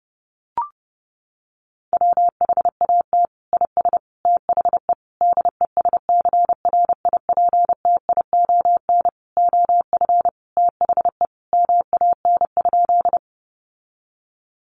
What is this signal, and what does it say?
Morse code, heard as a keyed tone, about 30 wpm, 700 Hz. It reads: WHAT IS THE DESCRIPTION OF THE MAN?
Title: Sentences from Top 1300 Words 30wpm